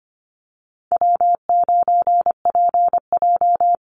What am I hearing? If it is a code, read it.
Morse code, heard as a keyed tone, about 25 wpm, 700 Hz. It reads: W9PJ
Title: Easy Call Signs - 25wpm